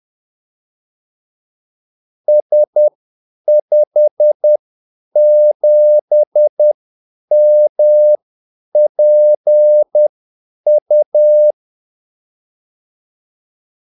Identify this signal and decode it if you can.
Morse code, heard as a keyed tone, about 10 wpm, spaced as for 8 wpm, 600 Hz. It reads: S57MPU
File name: Callsigns-10-8wpm